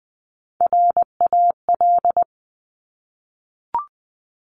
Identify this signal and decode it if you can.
Morse code, heard as a keyed tone, about 20 wpm, 700 Hz. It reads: RAL
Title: Top Three Letter Combinations in Words 20wpm